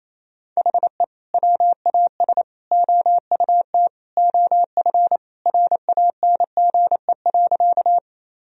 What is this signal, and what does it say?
Morse code, heard as a keyed tone, about 28 wpm, 700 Hz. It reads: HE WAS OUT OF RANGE.
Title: Sentences from Top 1000 Words - Review Part 4 - 28wpm